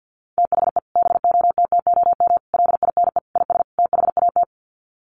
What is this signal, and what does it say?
Morse code, heard as a keyed tone, about 50 wpm, 700 Hz. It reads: THE BOTTOM LINE IS THAT
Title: Sets of 5 Words Encore 50wpm